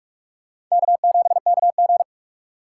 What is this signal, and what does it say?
Morse code, heard as a keyed tone, about 48 wpm, 700 Hz. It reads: K7KG
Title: USA Call Signs - 4 Characters - 3x Character Spacing - 45wpm